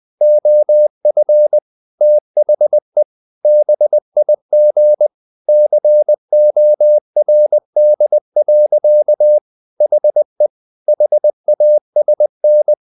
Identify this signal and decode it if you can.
Morse code, heard as a keyed tone, about 20 wpm, 600 Hz. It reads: OF THE BIG CORD. HE HASN
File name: the old man and the sea part 2 20wpm